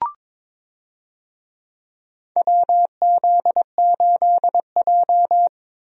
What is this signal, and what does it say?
Morse code, heard as a keyed tone, about 22 wpm, 700 Hz. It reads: WZ8J
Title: USA Call Signs - 4 Characters - Repeated 2x - 22wpm